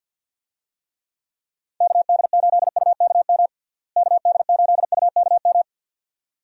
Morse code, heard as a keyed tone, about 47 wpm, 700 Hz. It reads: KD8UKM KD8UKM
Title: USA Call Signs - 6 Characters - Repeated 2x - 50wpm